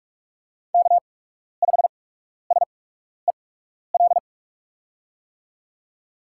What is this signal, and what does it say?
Morse code, heard as a keyed tone, about 47 wpm, spaced as for 12 wpm, 700 Hz. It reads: K5SEL
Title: USA Call Signs - 5 Characters - 8x Character Spacing - 45wpm